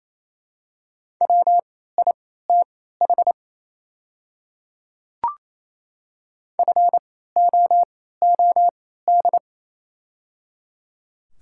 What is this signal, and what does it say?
Morse code, heard as a keyed tone, about 28 wpm, spaced as for 16 wpm, 700 Hz. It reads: WITH FOOD
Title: 3x Character Spacing - Top 300 Words - 28wpm